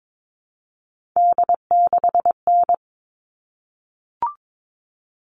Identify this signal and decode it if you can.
Morse code, heard as a keyed tone, about 22 wpm, 700 Hz. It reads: D6N
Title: All Cut Numbers - 3-Digit 22wpm